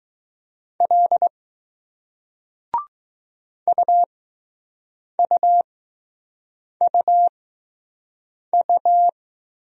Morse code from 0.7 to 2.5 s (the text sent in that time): L